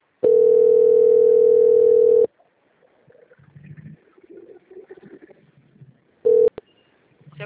Ringback tone (North America), 440 + 480 Hz.